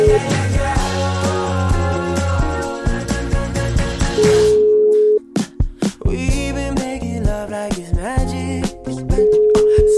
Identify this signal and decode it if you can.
Ringback tone (Europe, 425 Hz), with noise.